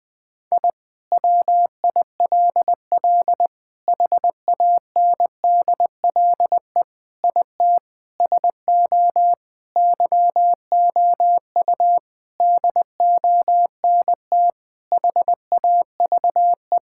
Morse code, heard as a keyed tone, about 20 wpm, 700 Hz. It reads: I WILL HANDLE IT SO YOU DONT HAVE
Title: Sentences with 7-Word Phrases - 20wpm